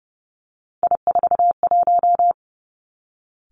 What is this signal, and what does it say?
Morse code, heard as a keyed tone, about 30 wpm, 700 Hz. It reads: I41
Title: Three Challenging Letter-Number Combinations 30wpm